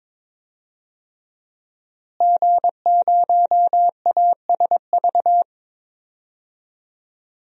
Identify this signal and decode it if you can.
Morse code, heard as a keyed tone, about 22 wpm, 700 Hz. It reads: G0ASV
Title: DXCC-100 Call Signs - 5 Characters - Repeated 3x - 22wpm